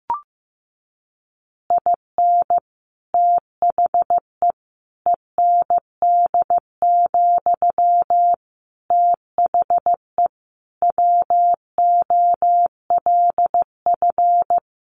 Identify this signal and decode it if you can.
Morse code, heard as a keyed tone, about 15 wpm, 700 Hz. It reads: IN THE END, THE WOLF
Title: Additional Easy Phrases 15wpm